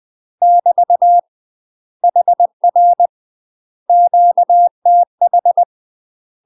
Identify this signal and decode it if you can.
Morse code, heard as a keyed tone, about 20 wpm, 700 Hz. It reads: <BT> HR QTH
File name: UK MASTER QSOs TRAINING / 20WPM / UK MASTER QSOs 0019